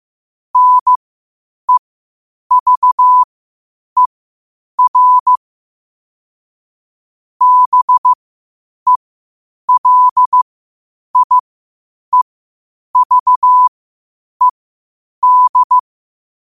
Morse code, heard as a keyed tone, about 15 wpm, spaced as for 8 wpm, 1000 Hz. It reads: NEVER BELIEVED
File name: qotdfiles / 8wpm